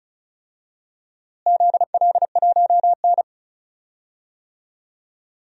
Morse code, heard as a keyed tone, about 36 wpm, 700 Hz. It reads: ZL1N
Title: DXCC-100 Call Signs - 4 Characters - Repeated 2x - 35wpm